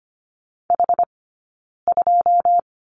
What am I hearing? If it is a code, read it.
Morse code, heard as a keyed tone, about 24 wpm, 700 Hz. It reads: H 2
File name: Lesson 15d - Number 5 - Two Characters 25wpm